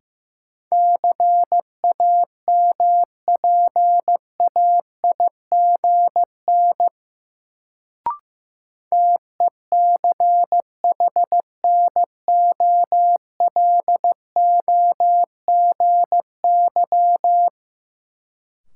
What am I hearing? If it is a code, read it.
Morse code, heard as a keyed tone, about 15 wpm, 700 Hz. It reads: CAMPAIGN TECHNOLOGY
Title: Sentences from Top 1200 Words - New Word Drill Alt - 15wpm